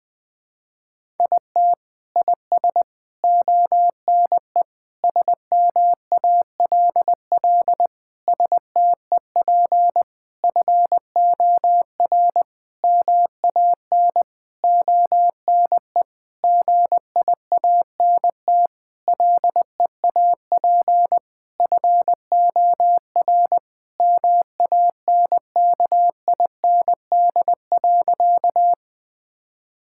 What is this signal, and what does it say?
Morse code, heard as a keyed tone, about 20 wpm, 700 Hz. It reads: IT IS ONE SMALL STEP FOR MAN ONE GIANT LEAP FOR MANKIND.
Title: Sentences with 9-Word Phrases - 20wpm